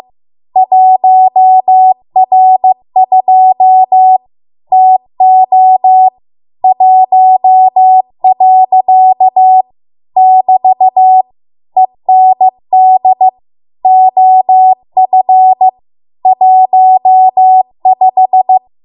Morse code, heard as a keyed tone, about 15 wpm, 750 Hz. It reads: 1R2 TO 1. <BT> END OF 15